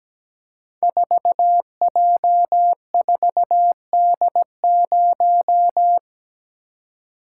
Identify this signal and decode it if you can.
Morse code, heard as a keyed tone, about 17 wpm, 700 Hz. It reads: 4J4D0
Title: Code Groups - 5 Letter:Numbers - 17wpm